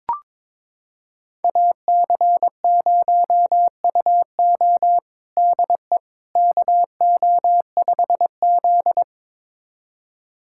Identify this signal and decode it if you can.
Morse code, heard as a keyed tone, about 22 wpm, 700 Hz. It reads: AC0UO DE KO5Z
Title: USA Callsign DE Callsign 22wpm